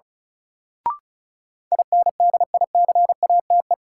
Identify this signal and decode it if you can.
Morse code, heard as a keyed tone, about 36 wpm, 700 Hz. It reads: INDICATE
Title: Sentences from Top 1300 Words - New Word Drill - 35wpm